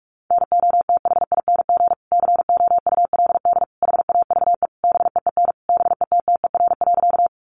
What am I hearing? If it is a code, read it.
Morse code, heard as a keyed tone, about 44 wpm, 700 Hz. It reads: NOTHING COULD HAVE BEEN BETTER.
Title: Sentences from Top 500 Words 45wpm